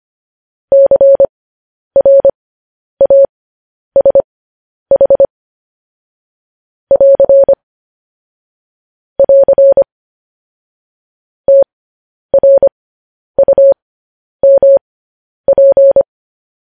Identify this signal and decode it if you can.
Morse code, heard as a keyed tone, about 25 wpm, spaced as for 10 wpm, 550 Hz. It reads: CRASH <AR> <AR> TRUMP
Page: news headlines in morse code at 10 wpm